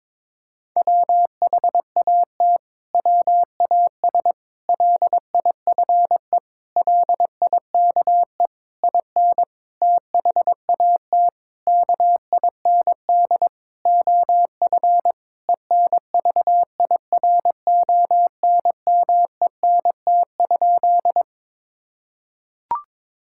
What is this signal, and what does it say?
Morse code, heard as a keyed tone, about 22 wpm, 700 Hz. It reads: WHAT WAS LIFE LIKE IN THAT KIND OF ENVIRONMENT?
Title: Sentences from Top 1000 Words - Review Part 5 - 22wpm